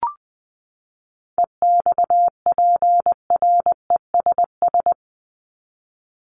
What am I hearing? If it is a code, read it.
Morse code, heard as a keyed tone, about 20 wpm, 700 Hz. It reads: EXPRESS
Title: Top 1001-1250 Words 20wpm